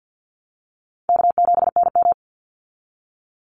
Morse code, heard as a keyed tone, about 50 wpm, 700 Hz. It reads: K7NM
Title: Call Signs Repeated 2x - 50wpm